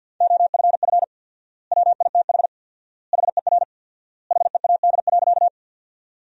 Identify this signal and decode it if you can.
Morse code, heard as a keyed tone, about 48 wpm, 700 Hz. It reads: OFF WITH HER HEAD.
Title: 4x Word-Spacing - Sentences from Top 300 Words - 50wpm